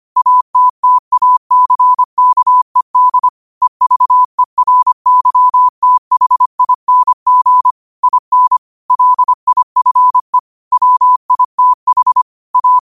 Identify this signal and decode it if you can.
Morse code, heard as a keyed tone, about 25 wpm, 1000 Hz. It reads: ATTACKED EVERYTHING IN LIFE WITH A